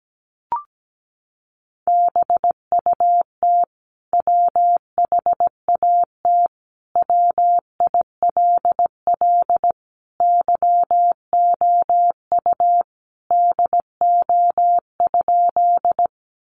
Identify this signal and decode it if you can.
Morse code, heard as a keyed tone, about 17 wpm, 700 Hz. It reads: BUT WHAT WILL YOU DO?
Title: Sentences from Top 100 Words 17wpm